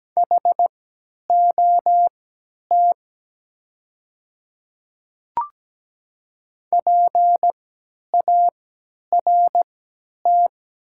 Morse code, heard as a keyed tone, about 17 wpm, spaced as for 10 wpm, 700 Hz. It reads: HOT PART
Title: 3x Character Spacing - Top 200 Words - 17wpm